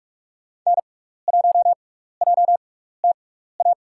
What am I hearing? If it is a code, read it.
Morse code, heard as a keyed tone, about 43 wpm, spaced as for 15 wpm, 700 Hz. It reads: N1JTA
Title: USA Call Signs - 5 Characters - 6x Character Spacing - 45wpm